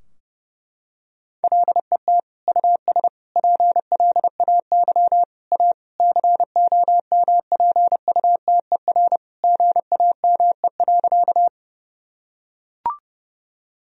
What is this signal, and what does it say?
Morse code, heard as a keyed tone, about 30 wpm, 700 Hz. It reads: LET US PLAY A COMPUTER GAME.
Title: Sentences from Top 1000 Words - Review Part 3 - 30wpm